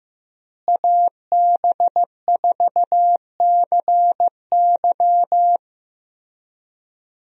Morse code, heard as a keyed tone, about 16 wpm, 700 Hz. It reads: AB4CY
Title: USA Call Signs - 5 Characters - Repeated 2x - 15wpm